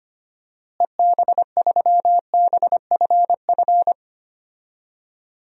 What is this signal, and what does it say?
Morse code, heard as a keyed tone, about 25 wpm, 700 Hz. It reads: EB3BFF
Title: DXCC-100 Call Signs - 6 Characters - Repeated 3x - 25wpm